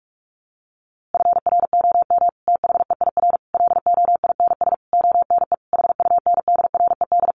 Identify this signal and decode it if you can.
Morse code, heard as a keyed tone, about 44 wpm, 700 Hz. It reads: FROM THEIR LOINS ONE HUNDRED
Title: A Sound Of Thunder 45wpm